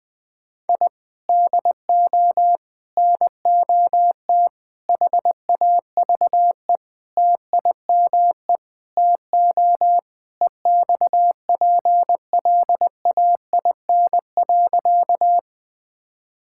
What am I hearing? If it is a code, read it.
Morse code, heard as a keyed tone, about 21 wpm, 700 Hz. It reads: I DO NOT HAVE TIME TO EXPLAIN.
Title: Sentences with 5-Word Phrases - Spoken Beginning - 20wpm